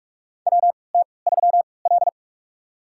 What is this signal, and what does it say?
Morse code, heard as a keyed tone, about 46 wpm, spaced as for 25 wpm, 700 Hz. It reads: WT3L
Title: USA Call Signs - 4 Characters - 3x Character Spacing - 45wpm